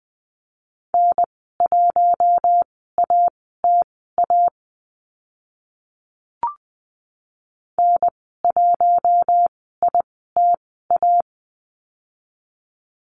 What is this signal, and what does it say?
Morse code, heard as a keyed tone, about 20 wpm, spaced as for 14 wpm, 700 Hz. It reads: N1ATA N1ITA
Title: 042 Lesson 8g - Number 1 - Call Signs 20wpm